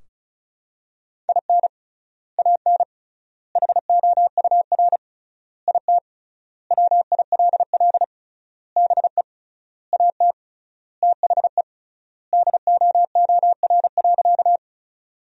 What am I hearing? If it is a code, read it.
Morse code, heard as a keyed tone, about 36 wpm, 700 Hz. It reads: IN AN HOUR IT WILL BE AT THE DOOR.
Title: 3x Word-Spacing - Sentences from Top 400 Words - 35wpm